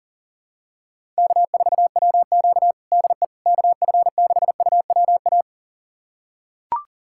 Morse code, heard as a keyed tone, about 40 wpm, 700 Hz. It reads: K4WQ DE KF6UWA
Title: USA Callsign DE Callsign 40wpm